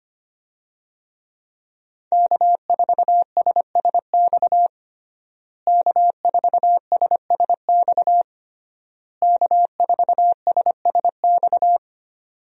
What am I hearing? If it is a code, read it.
Morse code, heard as a keyed tone, about 25 wpm, 700 Hz. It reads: K4SSX K4SSX K4SSX
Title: USA Call Signs - 5 Characters - Repeated 3x - 25wpm